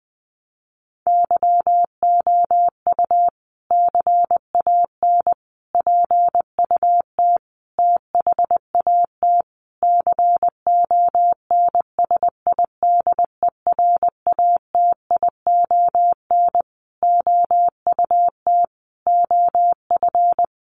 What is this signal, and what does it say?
Morse code, heard as a keyed tone, about 20 wpm, 700 Hz. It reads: YOU CAN PUT THAT CONSIDERATION OUT OF
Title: Sentences from Top 1500 Words 20wpm